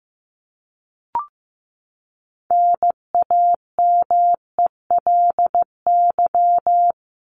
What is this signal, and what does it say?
Morse code, heard as a keyed tone, about 15 wpm, 700 Hz. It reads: NAMELY
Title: LY Suffix - Top 100 Words with Base Words 15wpm